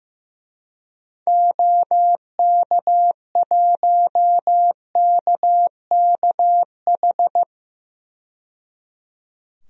Morse code, heard as a keyed tone, about 15 wpm, 700 Hz. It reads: OK1KKH
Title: DXCC-100 Call Signs - 6 Characters - 15wpm